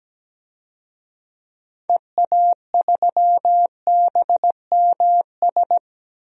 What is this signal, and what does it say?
Morse code, heard as a keyed tone, about 17 wpm, 700 Hz. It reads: EA3BMS
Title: DXCC-100 Call Signs - 6 Characters - Repeated 2x - 17wpm